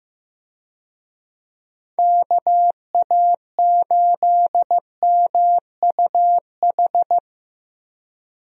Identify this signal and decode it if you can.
Morse code, heard as a keyed tone, about 15 wpm, 700 Hz. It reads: KA8MUH
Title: USA Call Signs - 6 Characters - Repeated 3x - 15wpm